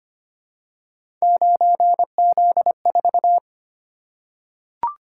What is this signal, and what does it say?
Morse code, heard as a keyed tone, about 25 wpm, 700 Hz. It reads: 9Z4
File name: Three Letter-Number Combinations - 25wpm